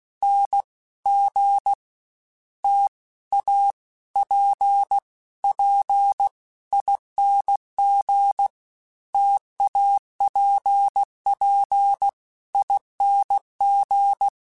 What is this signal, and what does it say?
Morse code, heard as a keyed tone, about 16 wpm, 800 Hz. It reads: N G T A P P ING TAPP ING
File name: Difficult Combos / Difficult Combos-mp3s / T AOEI @15